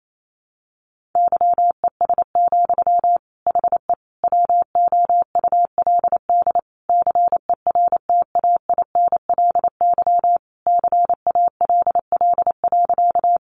Morse code, heard as a keyed tone, about 28 wpm, 700 Hz. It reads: YES, HE WOULD CERTAINLY CALL.
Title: Sentences from Top 900 Words 28wpm